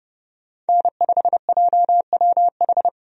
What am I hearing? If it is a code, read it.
Morse code, heard as a keyed tone, about 31 wpm, 700 Hz. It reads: N5JWH
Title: USA Call Signs - 5 Characters - 30wpm